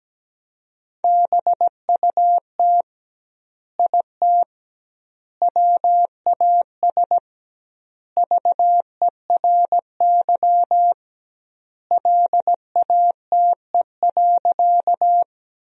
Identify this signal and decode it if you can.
Morse code, heard as a keyed tone, about 17 wpm, 700 Hz. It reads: BUT IT WAS VERY LATE.
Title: 2x Word-Spacing - Sentences from Top 300 Words - 17wpm